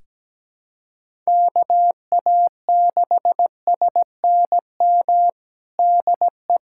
Morse code, heard as a keyed tone, about 17 wpm, 700 Hz. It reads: KA6SNM DE
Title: USA Callsign DE Callsign - 6 Characters - 17wpm